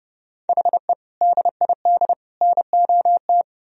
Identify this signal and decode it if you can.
Morse code, heard as a keyed tone, about 37 wpm, 700 Hz. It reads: HE DID NOT
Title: Speed-Racing - Sets of 3 Words 30wpm